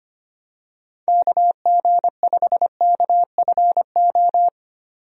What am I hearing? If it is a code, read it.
Morse code, heard as a keyed tone, about 25 wpm, 700 Hz. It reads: KG5KFO
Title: USA Call Signs - 6 Characters - 25wpm